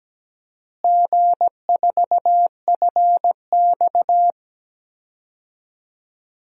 Morse code, heard as a keyed tone, about 17 wpm, 700 Hz. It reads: G4FX
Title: DXCC-100 Call Signs - 4 Characters - Repeated 2x - 17wpm